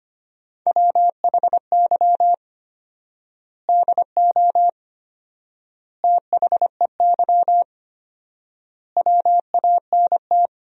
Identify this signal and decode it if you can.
Morse code, heard as a keyed tone, about 25 wpm, 700 Hz. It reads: WHY DO THEY WANT
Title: 4x Word-Spacing - Sentences from Top 400 Words - 25wpm